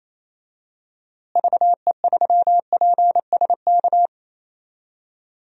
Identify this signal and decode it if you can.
Morse code, heard as a keyed tone, about 27 wpm, 700 Hz. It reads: VE3PSK
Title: DXCC-100 Call Signs - 6 Characters - Repeated 3x - 28wpm